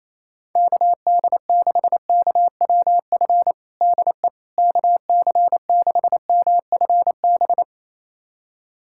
Morse code, heard as a keyed tone, about 28 wpm, 700 Hz. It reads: KD6KWF DE KC6MFB
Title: USA Callsign DE Callsign - 6 Characters - 28wpm